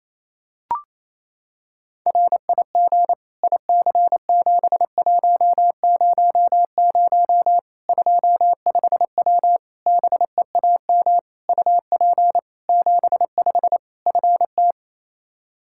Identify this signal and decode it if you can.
Morse code, heard as a keyed tone, about 28 wpm, 700 Hz. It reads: RIG IC7100 25W BEAM UP 75 FT
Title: Practice QSO - Exchange 2 - Info - 28wpm